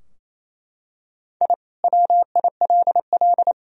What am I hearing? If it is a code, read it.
Morse code, heard as a keyed tone, about 28 wpm, 700 Hz. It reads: I WILL
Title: Sentences from Top 1000 Words - Review Part 3 - 28wpm